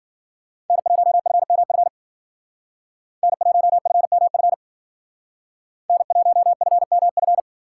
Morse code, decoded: N1FMF N1FMF N1FMF

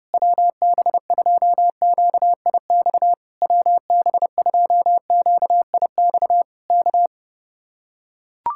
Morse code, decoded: WB2QIX WB2QIX K

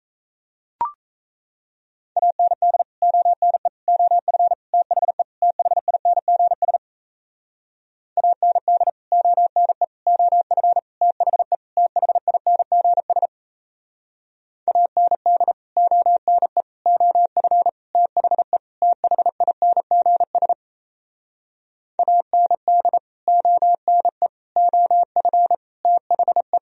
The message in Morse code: AND ONE OF THE THINGS AND ONE OF THE THINGS AND ONE OF THE THINGS AND ONE OF THE

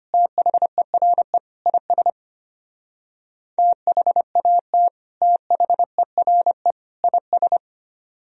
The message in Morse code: THERE IS THAT THERE IS

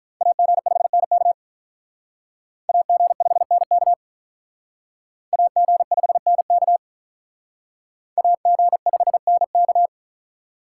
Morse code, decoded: AG5NK AG5NK AG5NK AG5NK

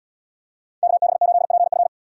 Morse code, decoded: KB8PF

